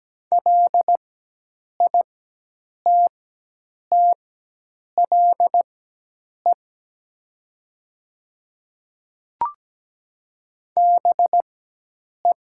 Morse code, heard as a keyed tone, about 17 wpm, spaced as for 8 wpm, 700 Hz. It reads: LITTLE BE
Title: 4x Character Spacing - Top 200 Words - 17wpm